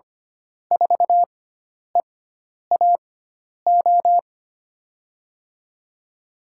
Morse code, heard as a keyed tone, about 25 wpm, 700 Hz. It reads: 4 E A O
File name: Lesson 16g - Letter c - Call Signs 25wpm